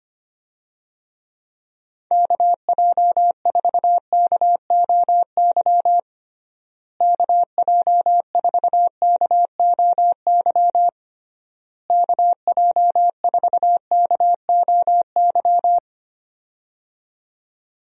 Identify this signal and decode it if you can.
Morse code, heard as a keyed tone, about 25 wpm, 700 Hz. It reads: KJ4KOY KJ4KOY KJ4KOY
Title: USA Call Signs - 6 Characters - Repeated 3x - 25wpm